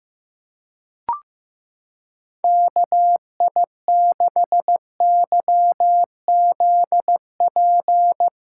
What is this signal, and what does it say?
Morse code, heard as a keyed tone, about 15 wpm, 700 Hz. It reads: KI6YZP
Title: USA Callsign DE Callsign 15wpm